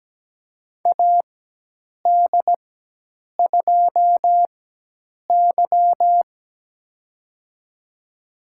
Morse code, heard as a keyed tone, about 17 wpm, spaced as for 8 wpm, 700 Hz. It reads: AD2Y
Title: USA Call Signs - 4x Character Spacing - 17wpm